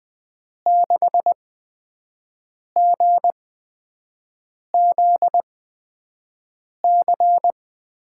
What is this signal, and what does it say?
Morse code, heard as a keyed tone, about 20 wpm, spaced as for 5 wpm, 700 Hz. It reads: 6GZC
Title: USA Call Signs - 6 Characters - 8x Character Spacing - 20wpm